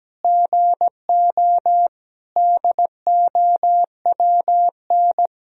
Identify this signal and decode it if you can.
Morse code, heard as a keyed tone, about 17 wpm, 700 Hz. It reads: GO DOWN
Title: Sentences from Top 1000 Words - Review Part 1 - 17wpm